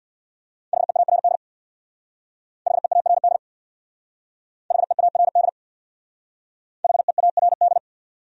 Morse code, heard as a keyed tone, about 62 wpm, 700 Hz. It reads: HEARD HEARD HEARD HEARD
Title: Speed-Racing Top 500 Words 50wpm